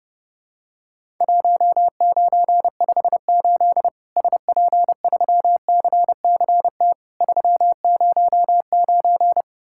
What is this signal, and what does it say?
Morse code, heard as a keyed tone, about 30 wpm, 700 Hz. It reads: 1958 SP3CCT 309
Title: Worked All Europe DX Contest - QTC Exchange - 30wpm